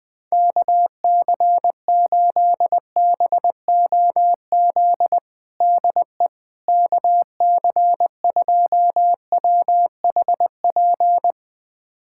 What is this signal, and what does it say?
Morse code, heard as a keyed tone, about 20 wpm, 700 Hz. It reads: KC8BOZ DE KC2WHP
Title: USA Callsign DE Callsign - 6 Characters - 20wpm